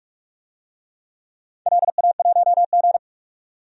Morse code, heard as a keyed tone, about 44 wpm, 700 Hz. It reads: LA1G
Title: DXCC-100 Call Signs - 4 Characters - 45wpm